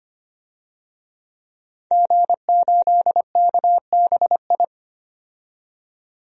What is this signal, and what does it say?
Morse code, heard as a keyed tone, about 25 wpm, 700 Hz. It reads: G8KBI
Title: DXCC-100 Call Signs - 5 Characters - 25wpm